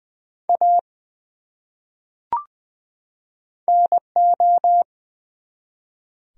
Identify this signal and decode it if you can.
Morse code, heard as a keyed tone, about 20 wpm, 700 Hz. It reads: A NO
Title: Top 100 Words 20wpm